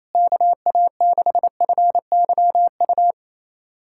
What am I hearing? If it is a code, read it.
Morse code, heard as a keyed tone, about 28 wpm, 700 Hz. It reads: KA6FYU
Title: USA Callsign DE Callsign - 6 Characters - 28wpm